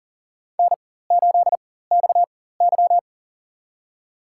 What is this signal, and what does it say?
Morse code, heard as a keyed tone, about 40 wpm, spaced as for 19 wpm, 700 Hz. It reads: N8XY
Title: USA Call Signs - 4 Characters - 4x Character Spacing - 40wpm